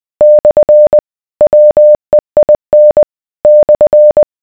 Morse code, heard as a keyed tone, about 20 wpm, 600 Hz. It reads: / WEIN /